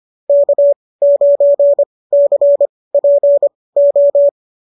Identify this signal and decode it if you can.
Morse code, heard as a keyed tone, about 25 wpm, spaced as for 18 wpm, 590 Hz. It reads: K9CPO
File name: qso211 18